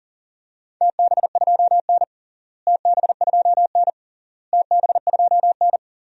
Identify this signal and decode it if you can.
Morse code, heard as a keyed tone, about 40 wpm, 700 Hz. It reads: TB2N TB2N TB2N
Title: DXCC-100 Call Signs - 4 Characters - Repeated 3x - 40wpm